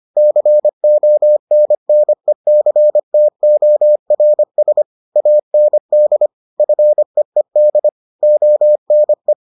Morse code, heard as a keyed tone, about 25 wpm, 600 Hz. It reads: CONNECTORS AND FEED ONE